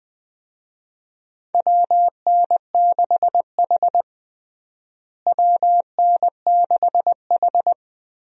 Morse code, heard as a keyed tone, about 20 wpm, 700 Hz. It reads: WN6H WN6H
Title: USA Call Signs - 4 Characters - Repeated 2x - 20wpm